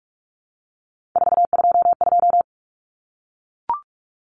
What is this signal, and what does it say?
Morse code, heard as a keyed tone, about 44 wpm, 700 Hz. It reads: 422
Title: Common Cut Numbers - 3-Digit 45wpm